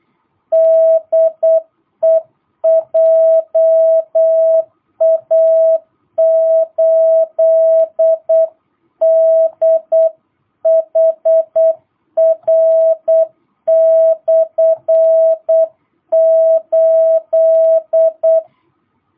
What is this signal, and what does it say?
Morse code, heard as a keyed tone, about 8 wpm, 650 Hz. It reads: DE JA8DHR/8
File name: jh8cbh / 2022cw